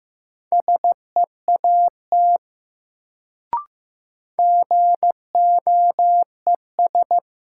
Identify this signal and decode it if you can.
Morse code, heard as a keyed tone, about 15 wpm, 700 Hz. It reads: SEAT GOES
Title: Sentences from Top 1500 Words - New Word Drill Alt - 15wpm